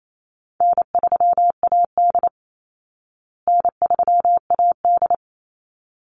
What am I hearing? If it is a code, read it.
Morse code, heard as a keyed tone, about 28 wpm, 700 Hz. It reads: N3AD N3AD
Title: Easy Call Signs Repeated 2x - 28wpm